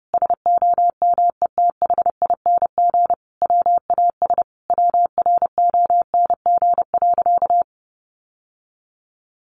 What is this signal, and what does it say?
Morse code, heard as a keyed tone, about 30 wpm, 700 Hz. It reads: SOMETHING WAS WRONG.
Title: Sentences from Top 1200 Words 30wpm